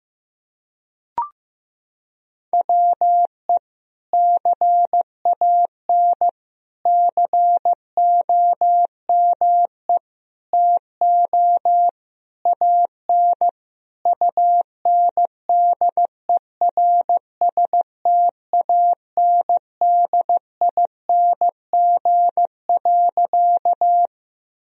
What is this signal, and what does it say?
Morse code, heard as a keyed tone, about 15 wpm, 700 Hz. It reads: WE CAN COME TO AN UNDERSTANDING.
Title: Sentences from Top 1000 Words - Review Part 3 - 15wpm